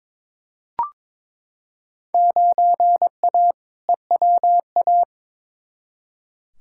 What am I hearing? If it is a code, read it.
Morse code, heard as a keyed tone, about 22 wpm, 700 Hz. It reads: 9A EWA
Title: Field Day Exchanges Encore - 22wpm